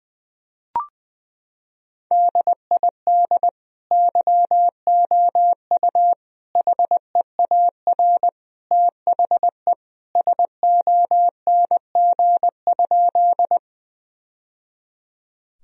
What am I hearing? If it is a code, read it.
Morse code, heard as a keyed tone, about 20 wpm, 700 Hz. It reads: DID YOU HEAR THE SONG?
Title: Sentences from Top 1000 Words - Review Part 2 - 20wpm